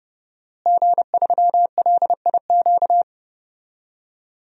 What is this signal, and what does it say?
Morse code, heard as a keyed tone, about 30 wpm, 700 Hz. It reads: G3LIQ
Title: DXCC-100 Call Signs - 5 Characters - Repeated 3x - 30wpm